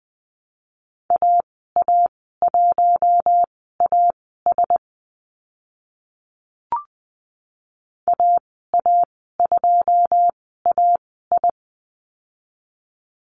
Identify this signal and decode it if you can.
Morse code, heard as a keyed tone, about 20 wpm, spaced as for 14 wpm, 700 Hz. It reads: AA1AS AA2AI
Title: 082 Lesson 14g - Number 2 - Call Signs 20wpm